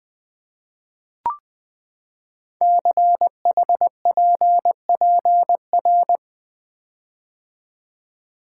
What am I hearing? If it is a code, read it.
Morse code, heard as a keyed tone, about 20 wpm, 700 Hz. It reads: CHPPR
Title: Code Groups - 5 Letters - 20wpm